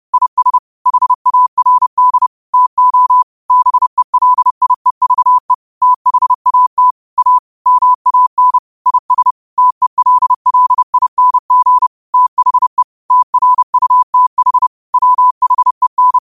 Morse code, heard as a keyed tone, about 30 wpm, 1000 Hz. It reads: IS HARD TO BELIEVE THAT A MAN IS TELLING THE TRUTH WHEN
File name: qotdfiles / 30wpm